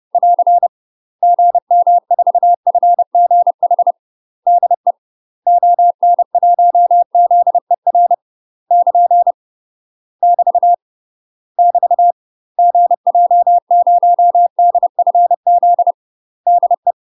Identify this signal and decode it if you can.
Morse code, heard as a keyed tone, about 30 wpm, 700 Hz. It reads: <AR> GM4FGH DE ON1ZER <KN> <BT> <BT> GJ0DFZ DE
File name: UK MASTER QSOs TRAINING / 30WPM / UK MASTER QSOs 0046